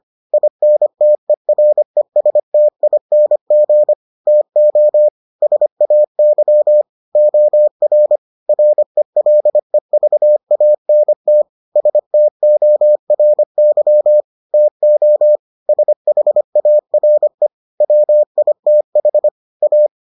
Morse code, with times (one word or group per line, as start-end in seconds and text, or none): 0.3-3.9 INTERESTING
4.3-5.1 TO
5.4-6.8 SAY
7.2-8.2 OR
8.5-11.4 RELEVANT
11.8-14.2 STORY
14.5-15.4 TO
15.7-17.5 SHARE
17.8-19.3 WITH
19.6-19.9 A